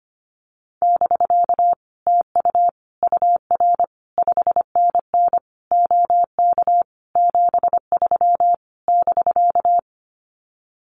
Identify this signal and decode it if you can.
Morse code, heard as a keyed tone, about 25 wpm, 700 Hz. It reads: <BK> TU UR 5NN OK 73 <BK>
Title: Common QSO Phrases Encore 25wpm